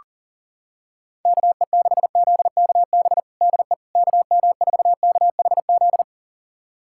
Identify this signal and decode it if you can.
Morse code, heard as a keyed tone, about 40 wpm, 700 Hz. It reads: KE6ZKB DE KM4KHZ
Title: USA Callsign DE Callsign 40wpm